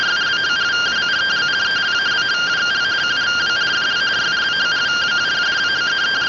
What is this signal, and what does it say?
Two-tone teleprinter signal, about 170 Hz shift.